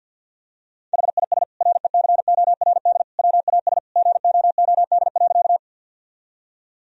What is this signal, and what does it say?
Morse code, heard as a keyed tone, about 50 wpm, 700 Hz. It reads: HIS RECORD WAS GOOD.
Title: Sentences from Top 1000 Words - Review Part 3 - 50wpm